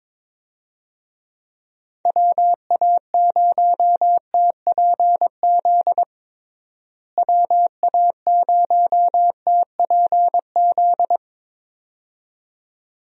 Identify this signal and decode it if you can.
Morse code, heard as a keyed tone, about 22 wpm, 700 Hz. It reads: WA0TPZ WA0TPZ
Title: USA Call Signs - 6 Characters - Repeated 2x - 22wpm